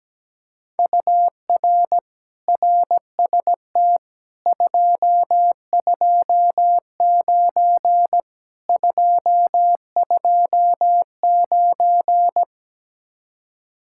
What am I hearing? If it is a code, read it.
Morse code, heard as a keyed tone, about 17 wpm, 700 Hz. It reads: UR RST 229 229
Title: Speed-Racing Common QSO Phrases Encore - 17wpm